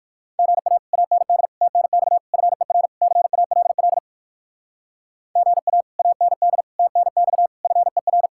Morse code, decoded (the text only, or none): GA AND TNX FER CALL GA AND TNX FER